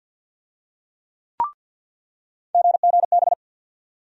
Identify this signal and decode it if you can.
Morse code, heard as a keyed tone, about 47 wpm, 700 Hz. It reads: GGB
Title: USA Call Sign Postfixes - 3 Letters 50wpm